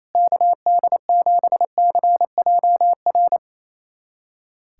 Morse code, heard as a keyed tone, about 28 wpm, 700 Hz. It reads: KD7CJR
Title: USA Callsign DE Callsign 28wpm